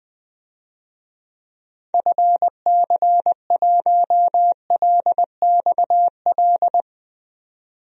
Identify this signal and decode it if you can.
Morse code, heard as a keyed tone, about 21 wpm, 700 Hz. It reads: FC1LXL
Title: DXCC-100 Call Signs - 6 Characters - 20wpm